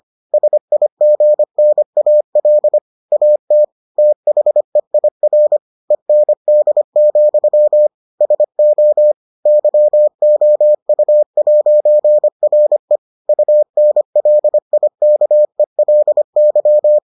Morse code, heard as a keyed tone, about 25 wpm, 600 Hz. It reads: SIGNAL AT THEIR END, SO YOU'RE UNLIKELY